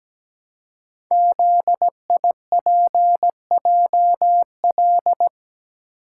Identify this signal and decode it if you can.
Morse code, heard as a keyed tone, about 17 wpm, 700 Hz. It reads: ZIPJL
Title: Code Groups - 5 Letter:Numbers - 17wpm